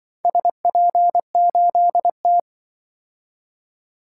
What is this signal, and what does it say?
Morse code, heard as a keyed tone, about 24 wpm, 700 Hz. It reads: SP8T
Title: Speed-Racing - Four Letter-Number Combinations - 20wpm